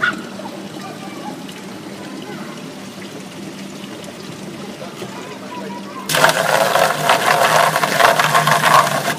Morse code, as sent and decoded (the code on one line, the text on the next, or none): none